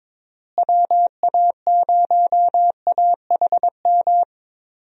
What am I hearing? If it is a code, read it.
Morse code, heard as a keyed tone, about 22 wpm, 700 Hz. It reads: WA0AHM
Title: USA Call Signs - 6 Characters - Repeated 3x - 22wpm